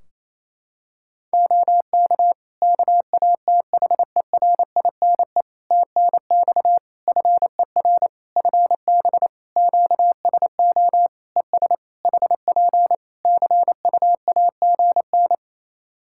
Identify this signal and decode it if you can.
Morse code, heard as a keyed tone, about 27 wpm, 700 Hz. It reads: OK KATHERINE TNX FER FB QSO ES HP CUAGN
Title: K1USN SST QSO Practice 28wpm